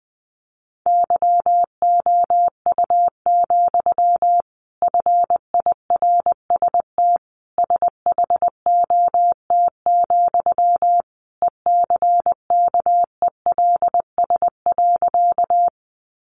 Morse code, heard as a keyed tone, about 20 wpm, 700 Hz. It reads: YOU, FIRST SHOT, ECKELS.
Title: A Sound Of Thunder 20wpm